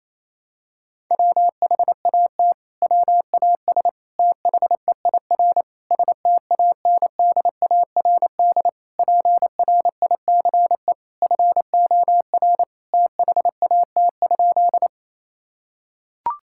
Morse code, heard as a keyed tone, about 28 wpm, 700 Hz. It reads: WHAT WAS THEIR STANDARD PRICE FOR THAT?
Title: Sentences from Top 1000 Words - Review Part 6 - 28wpm